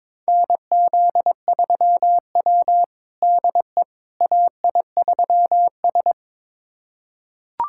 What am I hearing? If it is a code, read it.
Morse code, heard as a keyed tone, about 22 wpm, 700 Hz. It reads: NZ3W DE AI3S
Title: USA Callsign DE Callsign - 4 Characters - 22wpm